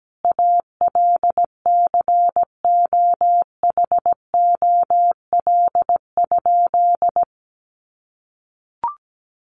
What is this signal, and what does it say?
Morse code, heard as a keyed tone, about 17 wpm, 700 Hz. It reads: ALCOHOL?
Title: Sentences from Top 1400 Words 17wpm